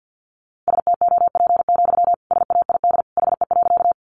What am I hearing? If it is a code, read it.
Morse code, heard as a keyed tone, about 50 wpm, 700 Hz. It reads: STOP, SAID HE.
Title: Sentences from Top 300 Words 50wpm